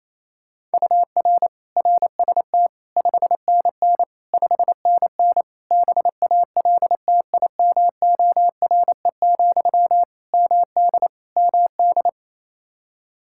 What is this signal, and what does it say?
Morse code, heard as a keyed tone, about 28 wpm, 700 Hz. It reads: UR RST 5NN 5NN BALTIMORE, MD MD
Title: Practice QSO - Exchange 1 - Beginning - 28wpm